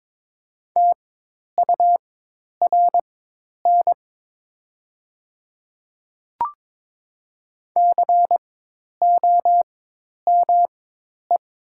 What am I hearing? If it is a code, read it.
Morse code, heard as a keyed tone, about 22 wpm, spaced as for 10 wpm, 700 Hz. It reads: TURN COME
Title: 4x Character Spacing - Top 200 Words - 22wpm